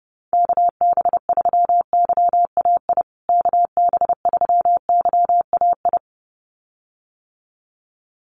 Morse code, heard as a keyed tone, about 30 wpm, 700 Hz. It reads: KB3YAI KB3YAI
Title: USA Call Signs Repeated 2x 30wpm